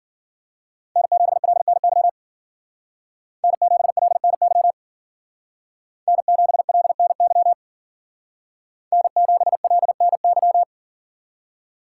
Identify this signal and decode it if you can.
Morse code, heard as a keyed tone, about 52 wpm, 700 Hz. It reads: N7LNY N7LNY N7LNY N7LNY